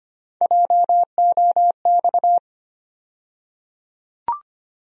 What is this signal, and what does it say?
Morse code, heard as a keyed tone, about 25 wpm, 700 Hz. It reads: JOX
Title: USA Call Sign Postfixes - 3 Letters 25wpm